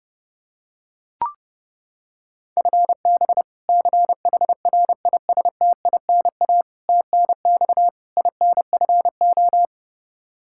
Morse code, heard as a keyed tone, about 30 wpm, 700 Hz. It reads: FB CHRISTINA TNX INFO
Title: Practice QSO - Exchange 3 - More Info - 30wpm